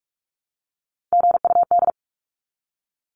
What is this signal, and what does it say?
Morse code, heard as a keyed tone, about 44 wpm, 700 Hz. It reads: GUD
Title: Abbreviations 45wpm